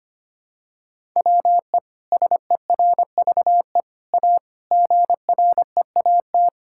Morse code, heard as a keyed tone, about 25 wpm, 700 Hz. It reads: WE SERVE A GREAT
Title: Sentences from Top 1000 Words - Review Part 6 - 25wpm